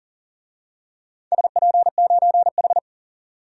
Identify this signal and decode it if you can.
Morse code, heard as a keyed tone, about 40 wpm, 700 Hz. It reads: SP9H
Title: DXCC-100 Call Signs - 4 Characters - 40wpm